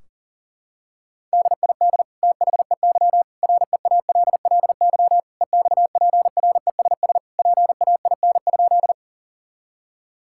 Morse code, heard as a keyed tone, about 40 wpm, 700 Hz. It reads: DID THEY REALLY EXPRESS PAIN?
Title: Sentences from Top 1600 Words 40wpm